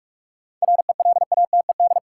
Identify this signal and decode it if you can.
Morse code, heard as a keyed tone, about 45 wpm, 700 Hz. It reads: RELATED
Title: Sentences from Top 1400 Words - New Word Drill Alt - 45wpm